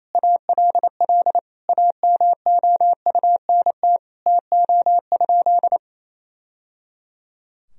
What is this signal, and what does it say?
Morse code, heard as a keyed tone, about 28 wpm, 700 Hz. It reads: ALL AMOUNT TO?
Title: Sentences from Top 1000 Words - Review Part 4 - 28wpm